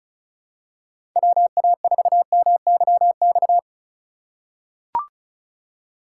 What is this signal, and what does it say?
Morse code, heard as a keyed tone, about 36 wpm, 700 Hz. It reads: WA4MYX
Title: USA Call Signs - 6 Characters - 3x Character Spacing - 35wpm